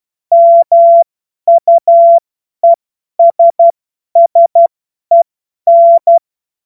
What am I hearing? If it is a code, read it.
Morse code, heard as a keyed tone, about 12 wpm, spaced as for 10 wpm, 680 Hz. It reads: MUESSEN
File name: Morsekurs / Witze / Witz 07. Juni 2022 50 F70